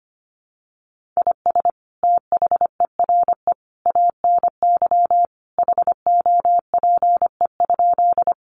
Morse code, heard as a keyed tone, about 25 wpm, 700 Hz. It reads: IS THERE ANY HOPE?
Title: Sentences from Top 1000 Words 25wpm